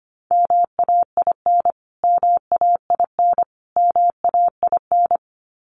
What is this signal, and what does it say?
Morse code, heard as a keyed tone, about 25 wpm, 700 Hz. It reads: MAIN MAIN MAIN
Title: Top 500 Words Repeated 25wpm